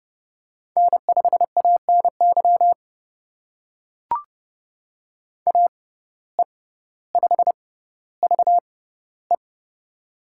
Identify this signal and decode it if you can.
Morse code, heard as a keyed tone, about 32 wpm, spaced as for 16 wpm, 700 Hz. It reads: N5ANY AE5VE
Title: USA Call Signs - 5 Characters - 6x Character Spacing - 30wpm